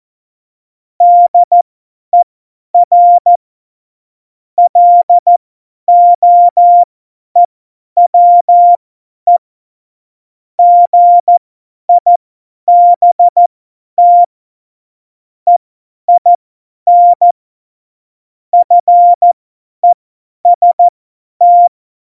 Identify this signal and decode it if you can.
Morse code, heard as a keyed tone, about 14 wpm, spaced as for 10 wpm, 700 Hz. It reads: DER LOEWE GIBT EIN FEST
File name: Morsekurs / Witze / Witz 23. Oktober 2018, 50 F70